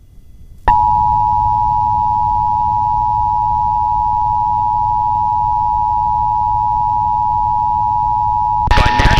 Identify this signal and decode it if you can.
Emergency Alert System (SAME) tones (the 853 and 960 Hz attention tone). It is heard from 0.7 to 8.7 s.